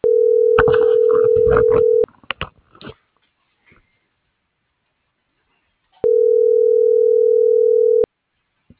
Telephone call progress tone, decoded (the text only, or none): ringback tone